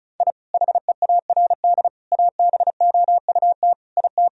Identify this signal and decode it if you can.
Morse code, heard as a keyed tone, about 34 wpm, 700 Hz. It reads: I HEARD ABOUT IT
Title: Sentences from Top 1000 Words - Review Part 1 - 35wpm